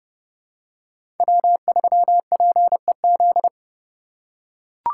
Morse code, heard as a keyed tone, about 29 wpm, 700 Hz. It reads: W3PEZ
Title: USA Call Signs - 2x Character Spacing - 30wpm